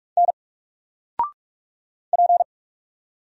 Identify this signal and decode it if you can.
Morse code, heard as a keyed tone, about 46 wpm, 700 Hz. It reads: N P
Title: Single Letter-Number 45wpm